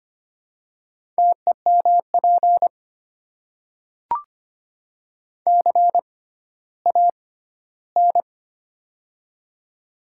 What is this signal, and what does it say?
Morse code, heard as a keyed tone, about 25 wpm, spaced as for 14 wpm, 700 Hz. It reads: TEMP CAN
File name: Top 90 QSO Elements - 6x Character Spacing - 25wpm